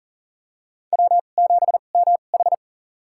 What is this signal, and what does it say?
Morse code, heard as a keyed tone, about 40 wpm, spaced as for 29 wpm, 690 Hz. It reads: W7MH
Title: USA Call Signs - 4 Characters - 2x Character Spacing - 40wpm